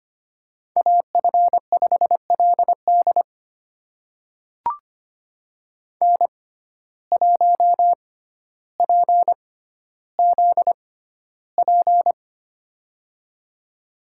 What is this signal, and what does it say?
Morse code, heard as a keyed tone, about 25 wpm, spaced as for 13 wpm, 700 Hz. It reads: AF5LD N1PZP
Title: USA Call Signs - 5 Characters - 6x Character Spacing - 25wpm